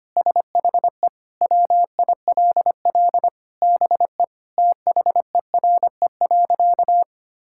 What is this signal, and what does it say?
Morse code, heard as a keyed tone, about 25 wpm, 700 Hz. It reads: SHE WILL BE THERE.
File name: practice / 25wpm / Sentences with 3-Word Phrases - 25wpm